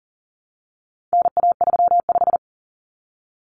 Morse code, heard as a keyed tone, about 40 wpm, 700 Hz. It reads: NA35
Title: Common Cut Numbers - 4-Digit 40wpm